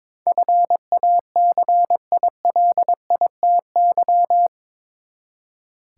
Morse code, heard as a keyed tone, about 22 wpm, 700 Hz. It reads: FACILITY
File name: practice / 22wpm / Eight Letter Words - 22wpm